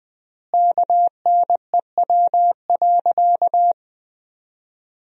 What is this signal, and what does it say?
Morse code, heard as a keyed tone, about 20 wpm, 700 Hz. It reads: KNEW.
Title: 3x Word-Spacing - Sentences from Top 400 Words - 20wpm